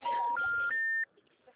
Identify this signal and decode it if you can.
Special information tone.